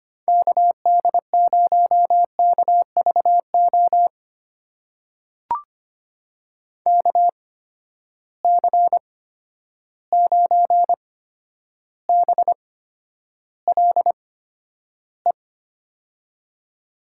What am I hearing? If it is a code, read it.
Morse code, heard as a keyed tone, about 25 wpm, spaced as for 11 wpm, 700 Hz. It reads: KD0KVO KC9BLE
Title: USA Call Signs - 8x Character Spacing - 25wpm